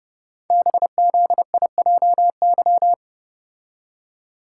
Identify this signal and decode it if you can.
Morse code, heard as a keyed tone, about 30 wpm, 690 Hz. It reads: BZIJY
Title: Code Groups - 5 Letters - 30wpm